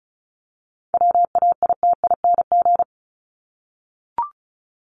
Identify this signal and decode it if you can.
Morse code, heard as a keyed tone, about 34 wpm, 700 Hz. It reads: WAITING
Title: ING Suffix - Top 100 Words with Base Words 35wpm